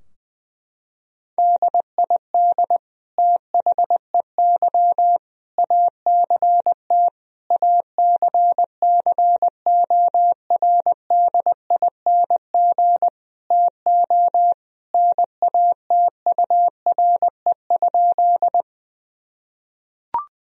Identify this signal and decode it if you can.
Morse code, heard as a keyed tone, about 20 wpm, 700 Hz. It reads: DID THEY ACT ACCORDING TO NATURE?
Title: Sentences from Top 1000 Words - Review Part 4 - 20wpm